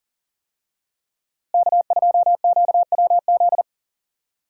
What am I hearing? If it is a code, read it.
Morse code, heard as a keyed tone, about 37 wpm, 700 Hz. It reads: K2QWZ
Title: USA Call Signs - 5 Characters - Repeated 2x - 40wpm